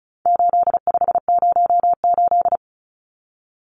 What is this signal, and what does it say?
Morse code, heard as a keyed tone, about 34 wpm, 700 Hz. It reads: 8508
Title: Numbers - 4-Digit 35wpm